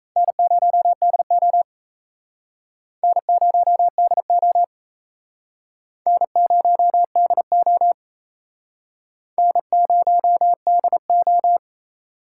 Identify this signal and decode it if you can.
Morse code, heard as a keyed tone, about 38 wpm, 700 Hz. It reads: N0DO N0DO N0DO N0DO